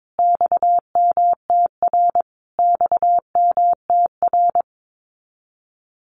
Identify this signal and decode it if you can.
Morse code, heard as a keyed tone, about 22 wpm, 700 Hz. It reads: XMTR XMTR
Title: Abbreviations Repeated 3x - 22wpm